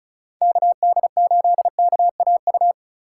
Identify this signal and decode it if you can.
Morse code, heard as a keyed tone, about 43 wpm, 700 Hz. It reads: KD8KAU
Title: Speed Racing - USA 2x3 Call Signs 35wpm